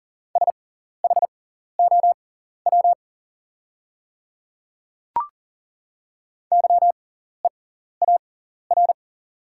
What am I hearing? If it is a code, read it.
Morse code, heard as a keyed tone, about 40 wpm, spaced as for 14 wpm, 690 Hz. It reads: SHOW YEAR
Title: 6x Character Spacing - Top 500 Words - 40wpm